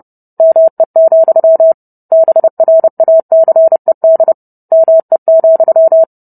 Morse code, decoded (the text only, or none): ME, BRACED ME,